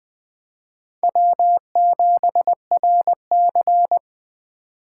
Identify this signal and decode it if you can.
Morse code, heard as a keyed tone, about 20 wpm, 700 Hz. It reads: W7RC
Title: Speed Racing - USA 1x2 Call Signs 17wpm